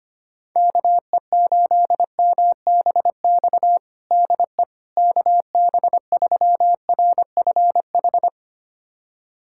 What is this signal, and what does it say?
Morse code, heard as a keyed tone, about 25 wpm, 700 Hz. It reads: KE8MBX DE KB3RFH
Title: USA Callsign DE Callsign 25wpm